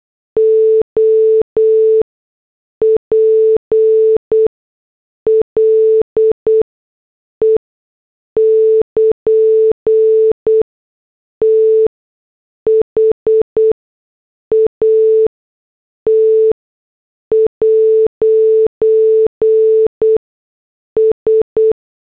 Morse code, heard as a keyed tone, about 8 wpm, spaced as for 6 wpm, 440 Hz. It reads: OPLE <KN> THAT'S
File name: morse / headlines